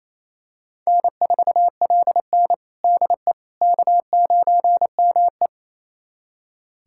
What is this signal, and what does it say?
Morse code, heard as a keyed tone, about 27 wpm, 700 Hz. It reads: N4LN DE K9ME
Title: USA Callsign DE Callsign - 4 Characters - 28wpm